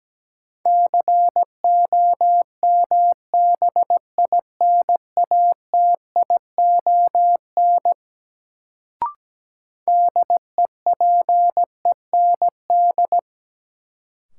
Morse code, heard as a keyed tone, about 17 wpm, 700 Hz. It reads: COMBINATION DEPEND
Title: Sentences from Top 1300 Words - New Word Drill Alt - 17wpm